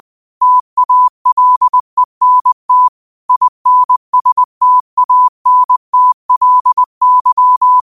Morse code, heard as a keyed tone, about 20 wpm, 1000 Hz. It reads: TALENT INSTANTLY
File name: qotdfiles / 20wpm